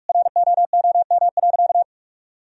Morse code, heard as a keyed tone, about 47 wpm, 700 Hz. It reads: ROOM.
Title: Sentences from Top 1000 Words - Review Part 4 - 45wpm